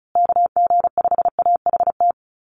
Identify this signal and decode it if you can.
Morse code, heard as a keyed tone, about 34 wpm, 700 Hz. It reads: KG5AHT
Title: USA Call Signs Repeated 3x 35wpm